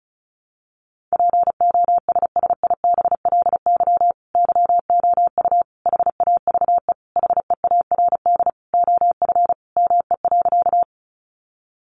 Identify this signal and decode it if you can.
Morse code, heard as a keyed tone, about 34 wpm, 700 Hz. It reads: POSSIBLY YOU HAVE HEARD OF ME.
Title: Sentences from Top 1300 Words 35wpm